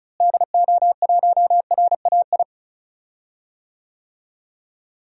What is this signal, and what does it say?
Morse code, heard as a keyed tone, about 34 wpm, 700 Hz. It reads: DO1RAI
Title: DXCC-100 Call Signs - 6 Characters - Repeated 2x - 35wpm